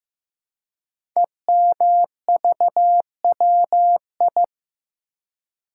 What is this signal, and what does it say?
Morse code, heard as a keyed tone, about 15 wpm, 700 Hz. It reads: EMVWI
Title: Code Groups - 5 Letters - 15wpm